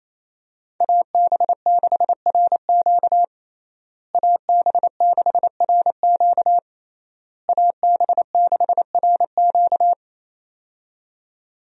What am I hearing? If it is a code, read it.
Morse code, heard as a keyed tone, about 28 wpm, 700 Hz. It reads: AB6RQ AB6RQ AB6RQ
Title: USA Call Signs - 5 Characters - Repeated 3x - 28wpm